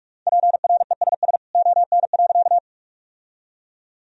Morse code, heard as a keyed tone, about 46 wpm, 700 Hz. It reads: PRESS ON.
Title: Sentences from Top 1000 Words - Review Part 1 - 45wpm